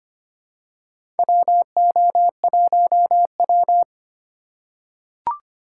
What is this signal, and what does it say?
Morse code, heard as a keyed tone, about 25 wpm, 700 Hz. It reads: WO1W
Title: USA Call Signs - 3x Character Spacing - 25wpm